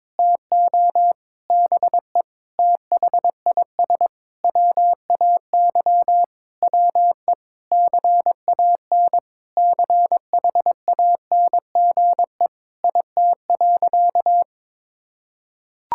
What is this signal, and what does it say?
Morse code, heard as a keyed tone, about 22 wpm, 700 Hz. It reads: TO BE THIS WAY WE CAN CHANGE IT.
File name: Sentences with 7-Word Phrases - 22wpm